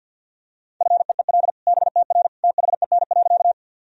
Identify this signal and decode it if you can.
Morse code, heard as a keyed tone, about 47 wpm, 700 Hz. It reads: FEEL BTR THEN.
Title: Sentences from Top 400 Words and Abbreviations 50wpm